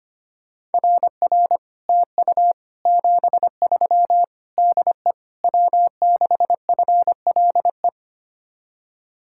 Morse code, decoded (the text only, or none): RR TU 73 DE W6FLE